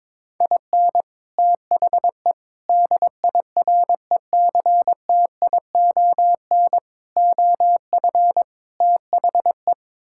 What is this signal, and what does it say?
Morse code, heard as a keyed tone, about 22 wpm, 700 Hz. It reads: IN THE DIRECTION OF THE
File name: practice / 22wpm / Build A Phrase - 5-Word Phrases - 22wpm